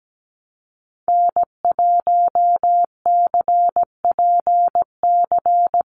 Morse code, heard as a keyed tone, about 17 wpm, 700 Hz. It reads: N1CPC
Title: USA Call Signs 17wpm